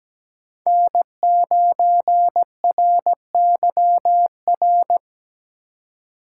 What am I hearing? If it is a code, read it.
Morse code, heard as a keyed tone, about 17 wpm, 700 Hz. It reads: N9RYR
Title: USA Call Signs - 5 Characters - 3x Character Spacing - 17wpm